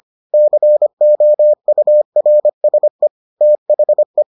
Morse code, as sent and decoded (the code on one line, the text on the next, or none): -.-. --- ..- .-. ... / - .... .
COURSE THE